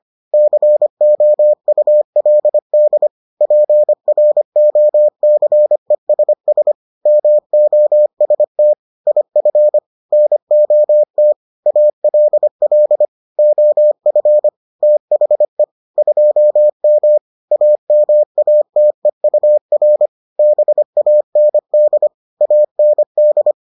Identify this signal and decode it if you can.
Morse code, heard as a keyed tone, about 25 wpm, 600 Hz. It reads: COULD PROCESS MOST IF NOT ALL OF THE 2M AMATEUR BAND AND